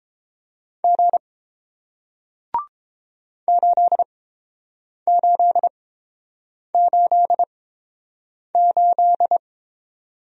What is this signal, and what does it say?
Morse code, heard as a keyed tone, about 30 wpm, spaced as for 7 wpm, 690 Hz. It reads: G 8888